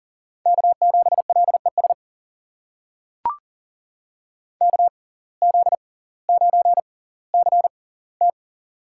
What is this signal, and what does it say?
Morse code, heard as a keyed tone, about 40 wpm, spaced as for 20 wpm, 700 Hz. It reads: K7LES KZ9CT
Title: USA Call Signs - 6x Character Spacing - 40wpm